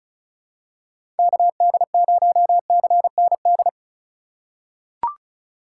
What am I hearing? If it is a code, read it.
Morse code, heard as a keyed tone, about 36 wpm, 700 Hz. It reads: KD0CND
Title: USA Call Signs - 6 Characters - 3x Character Spacing - 35wpm